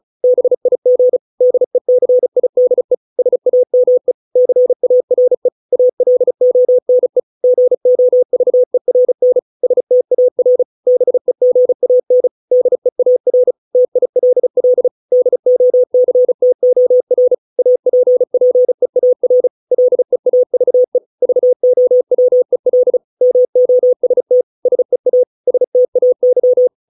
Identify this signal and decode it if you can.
Morse code, heard as a keyed tone, about 34 wpm, 500 Hz. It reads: BIG DECIDE SAME CARE ALONE GOVERN STAR BEGAN DEAR TILL DOCTOR APPEAR LEAVE VOWEL MOST SEA STAY